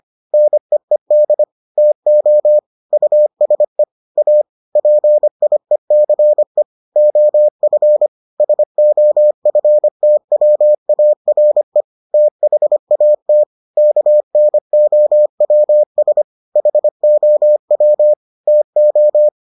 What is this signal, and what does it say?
Morse code, heard as a keyed tone, about 25 wpm, 600 Hz. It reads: NEED TO USE A PIECE OF SOFTWARE THAT KNOWS HOW TO